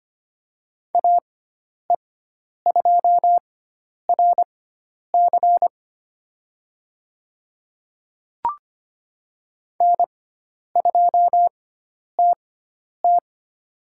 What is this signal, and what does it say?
Morse code, heard as a keyed tone, about 25 wpm, spaced as for 10 wpm, 700 Hz. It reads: AE2RC N2TT
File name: Lesson 16g - Letter c - Call Signs 25wpm